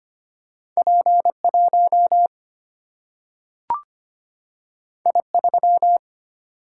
Morse code, heard as a keyed tone, about 25 wpm, 700 Hz. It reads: P1 I3
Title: Single-Letter Single-Number 25wpm